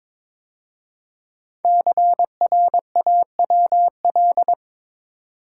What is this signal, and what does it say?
Morse code, heard as a keyed tone, about 22 wpm, 700 Hz. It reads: CRAWL
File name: 5 Letter Words - First 3 Letters Given - 22wpm